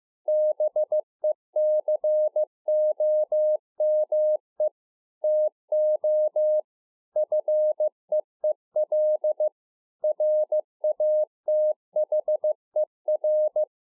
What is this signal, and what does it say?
Morse code, heard as a keyed tone, about 15 wpm, 610 Hz. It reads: BECOME TO FEEL RATHER
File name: 2011-04-mp3 / THE 3.11 6 15WPM